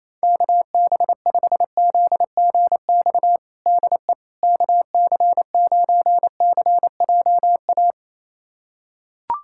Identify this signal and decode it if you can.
Morse code, heard as a keyed tone, about 27 wpm, 700 Hz. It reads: KB5ZGX DE KC9CJA
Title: USA Callsign DE Callsign - 6 Characters - 28wpm